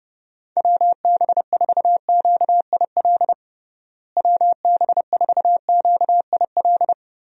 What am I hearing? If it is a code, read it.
Morse code, heard as a keyed tone, about 30 wpm, 700 Hz. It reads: WB4QIL WB4QIL
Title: USA Call Signs - 6 Characters - Repeated 3x - 30wpm